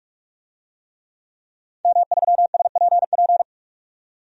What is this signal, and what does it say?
Morse code, heard as a keyed tone, about 43 wpm, 700 Hz. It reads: M3SPP
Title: DXCC-100 Call Signs - 5 Characters - 45wpm